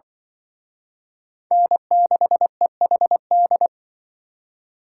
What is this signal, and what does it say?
Morse code, heard as a keyed tone, about 24 wpm, 700 Hz. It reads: N6EHD
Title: Speed Racing - USA Call Signs 20wpm